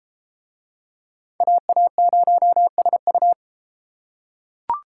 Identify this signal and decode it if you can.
Morse code, heard as a keyed tone, about 32 wpm, 700 Hz. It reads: AA0SU